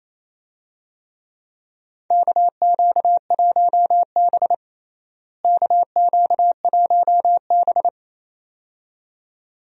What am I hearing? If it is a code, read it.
Morse code, heard as a keyed tone, about 28 wpm, 700 Hz. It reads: KQ1B KQ1B
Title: USA Call Signs - 4 Characters - Repeated 2x - 28wpm